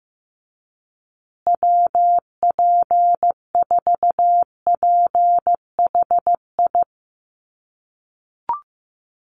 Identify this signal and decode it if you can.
Morse code, heard as a keyed tone, about 15 wpm, 700 Hz. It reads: WP4PHI
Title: USA Call Signs Repeated 3x 15wpm